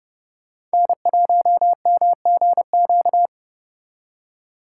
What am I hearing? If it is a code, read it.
Morse code, heard as a keyed tone, about 32 wpm, 700 Hz. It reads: N1MGQ